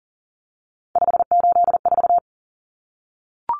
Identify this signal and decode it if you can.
Morse code, heard as a keyed tone, about 40 wpm, 700 Hz. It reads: 584
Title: Numbers - 3-Digit 40wpm